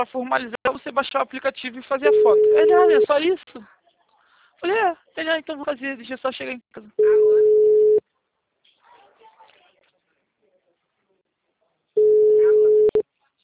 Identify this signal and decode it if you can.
Ringback tone (Europe, 425 Hz), with speech.